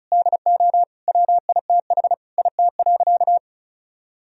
Morse code, heard as a keyed tone, about 36 wpm, 700 Hz. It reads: DO WITH IT.
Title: Sentences from Top 1000 Words - Review Part 2 - 35wpm